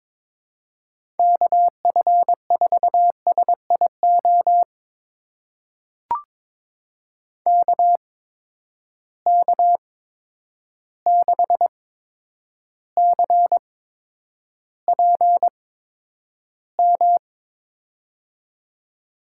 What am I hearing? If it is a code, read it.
Morse code, heard as a keyed tone, about 22 wpm, spaced as for 9 wpm, 700 Hz. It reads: KF4SIO KK6CPM
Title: USA Call Signs - 6 Characters - 8x Character Spacing - 22wpm